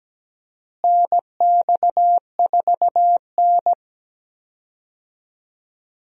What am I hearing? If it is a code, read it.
Morse code, heard as a keyed tone, about 17 wpm, 700 Hz. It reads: NX4N
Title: USA Call Signs - 4 Characters - Repeated 3x - 17wpm